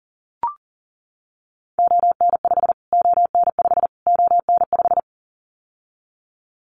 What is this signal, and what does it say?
Morse code, heard as a keyed tone, about 40 wpm, 700 Hz. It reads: ON5 ON5 ON5
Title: Callsign Prefixes for Top 25 DXCC - Repeated 3x - 40wpm